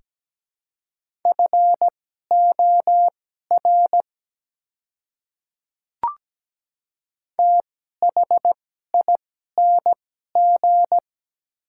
Morse code, heard as a keyed tone, about 17 wpm, spaced as for 12 wpm, 700 Hz. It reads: FOR THING
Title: 2x Character Spacing - Top 100 Words - 17wpm